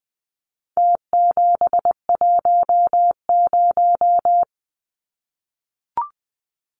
Morse code, heard as a keyed tone, about 20 wpm, 700 Hz. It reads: T710
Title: Numbers - Contest Style - 4-Digit 20wpm